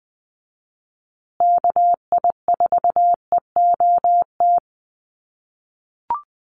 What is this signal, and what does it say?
Morse code, heard as a keyed tone, about 20 wpm, 700 Hz. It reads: KI4EOT
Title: USA Call Signs Repeated 3x 20wpm